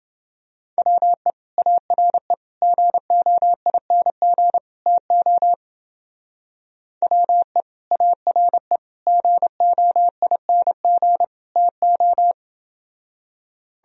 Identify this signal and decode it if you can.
Morse code, heard as a keyed tone, about 30 wpm, 700 Hz. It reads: WE ARE GOING TO WE ARE GOING TO